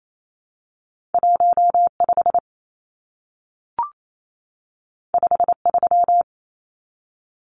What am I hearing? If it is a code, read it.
Morse code, heard as a keyed tone, about 28 wpm, 700 Hz. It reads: 15 53
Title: Numbers - 2-Digit 28wpm